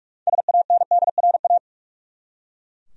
Morse code, heard as a keyed tone, about 42 wpm, 700 Hz. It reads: SANDRA
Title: USA Common Names - 45wpm